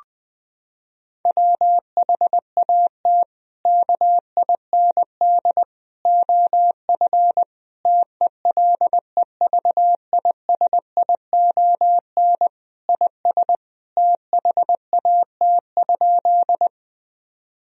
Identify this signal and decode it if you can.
Morse code, heard as a keyed tone, about 20 wpm, 700 Hz. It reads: WHAT KIND OF TELEVISION IS THAT?
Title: Sentences from Top 1000 Words - Review Part 4 - 20wpm